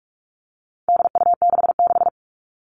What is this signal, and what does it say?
Morse code, heard as a keyed tone, about 44 wpm, 700 Hz. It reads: DU66
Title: All Cut Numbers - 4-Digit 45wpm